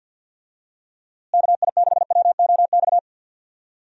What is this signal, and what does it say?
Morse code, heard as a keyed tone, about 51 wpm, 700 Hz. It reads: KI6WOX
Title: USA Call Signs - 6 Characters - Repeated 2x - 50wpm